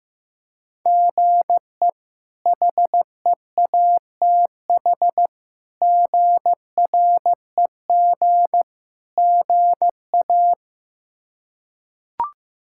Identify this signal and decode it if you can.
Morse code, heard as a keyed tone, about 16 wpm, 700 Hz. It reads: GE HEATH GREG GA
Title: K1USN SST QSO Practice - 15wpm